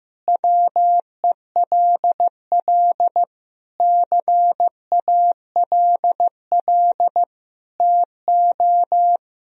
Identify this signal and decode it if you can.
Morse code, heard as a keyed tone, about 16 wpm, 700 Hz. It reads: WELL CALL TO
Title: Sentences from Top 1000 Words - Review Part 2 - 15wpm